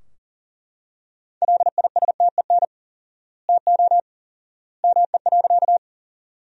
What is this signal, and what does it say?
Morse code, heard as a keyed tone, about 40 wpm, 700 Hz. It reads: LISTEN TO ME.
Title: 4x Word-Spacing - Sentences from Top 400 Words - 40wpm